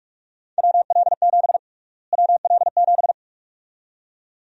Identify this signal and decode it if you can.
Morse code, heard as a keyed tone, about 43 wpm, 700 Hz. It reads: WL7 WL7
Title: USA Call Sign Prefixes - Repeated 2x - 45wpm